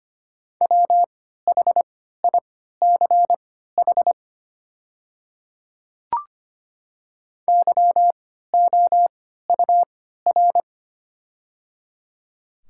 Morse code, heard as a keyed tone, about 25 wpm, spaced as for 14 wpm, 700 Hz. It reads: WHICH YOUR
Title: 3x Character Spacing - Top 200 Words - 25wpm